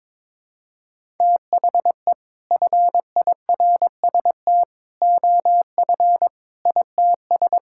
Morse code, heard as a keyed tone, about 22 wpm, 700 Hz. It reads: THE FIRST OF ITS
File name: practice / 22wpm / Sets of 6 Words - 22wpm